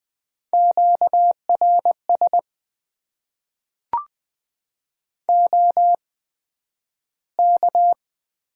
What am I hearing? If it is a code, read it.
Morse code, heard as a keyed tone, about 20 wpm, spaced as for 11 wpm, 700 Hz. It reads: QRS OK
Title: Top 90 QSO Elements - 8x Character Spacing - 20wpm